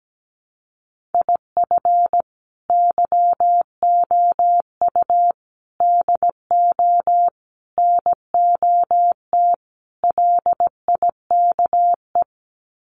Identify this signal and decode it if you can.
Morse code, heard as a keyed tone, about 17 wpm, 700 Hz. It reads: IF YOU DO NOT LIKE
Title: Sets of 5 Words 17wpm